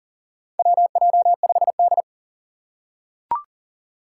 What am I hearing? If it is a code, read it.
Morse code, heard as a keyed tone, about 40 wpm, 700 Hz. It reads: WJ5D
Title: USA Call Signs - 4 Characters - 6x Character Spacing - 40wpm